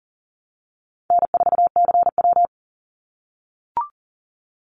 Morse code, heard as a keyed tone, about 40 wpm, 700 Hz. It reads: N4CW
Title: Easy Call Signs - 40wpm